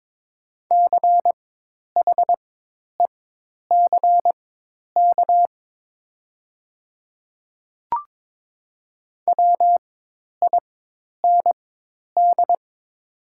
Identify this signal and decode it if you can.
Morse code, heard as a keyed tone, about 22 wpm, spaced as for 10 wpm, 700 Hz. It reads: CHECK WIND
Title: 4x Character Spacing - Top 500 Words - 22wpm